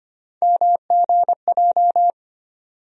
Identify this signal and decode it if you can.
Morse code, heard as a keyed tone, about 25 wpm, 700 Hz. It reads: MGJ
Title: USA Call Sign Postfixes - 3 Letters 25wpm